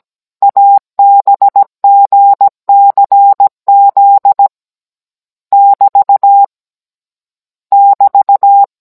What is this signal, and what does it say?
Morse code, heard as a keyed tone, about 17 wpm, 800 Hz. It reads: ABGCZ <BT> <BT>